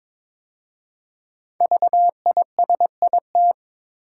Morse code, heard as a keyed tone, about 22 wpm, 700 Hz. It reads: VISIT
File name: 5 Letter Words - First 3 Letters Given - 22wpm